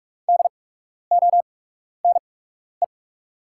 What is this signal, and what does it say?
Morse code, heard as a keyed tone, about 44 wpm, spaced as for 12 wpm, 690 Hz. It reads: DONE
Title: 8x Character Spacing - Top 500 Words - 45wpm